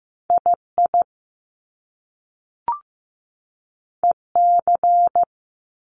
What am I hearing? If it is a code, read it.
Morse code, heard as a keyed tone, about 15 wpm, 700 Hz. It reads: II EC
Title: Top Two Letter Combinations in Words 15wpm